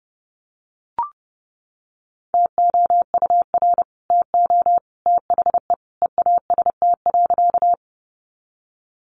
Morse code, heard as a keyed tone, about 30 wpm, 700 Hz. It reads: TOUR TO THE EAST.
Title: Sentences from Top 1400 Words 30wpm